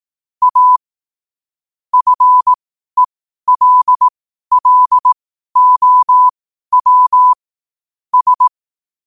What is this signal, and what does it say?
Morse code, heard as a keyed tone, about 18 wpm, spaced as for 12 wpm, 1000 Hz. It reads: A FELLOW S
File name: qotdfiles / 12wpm